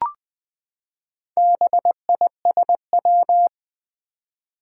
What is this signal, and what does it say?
Morse code, heard as a keyed tone, about 20 wpm, 700 Hz. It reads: BISW
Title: Four Challenging Letter-Number Combinations 20wpm